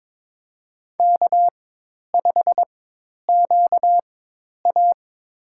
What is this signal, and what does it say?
Morse code, heard as a keyed tone, about 22 wpm, spaced as for 10 wpm, 700 Hz. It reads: K5QA
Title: USA Call Signs - 4 Characters - 4x Character Spacing - 22wpm